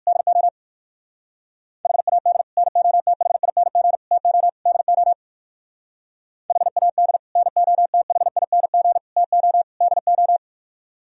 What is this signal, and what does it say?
Morse code, heard as a keyed tone, about 56 wpm, 700 Hz. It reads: DO HAD NOTHING TO DO HAD NOTHING TO DO
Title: Speed-Racing - Sets of 4 Words 45wpm